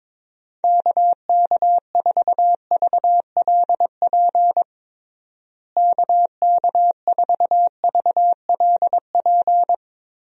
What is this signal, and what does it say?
Morse code, heard as a keyed tone, about 22 wpm, 700 Hz. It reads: KK4VLP KK4VLP
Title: USA Call Signs - 6 Characters - Repeated 3x - 22wpm